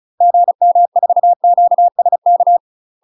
Morse code, heard as a keyed tone, about 36 wpm, 700 Hz. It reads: GM4QSK
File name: UK MASTER QSOs TRAINING / 35WPM / UK MASTER QSOs 0016